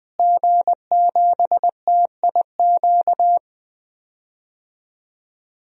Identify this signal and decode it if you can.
Morse code, heard as a keyed tone, about 20 wpm, 700 Hz. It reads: G7TIQ
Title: DXCC-100 Call Signs - 5 Characters - Repeated 3x - 20wpm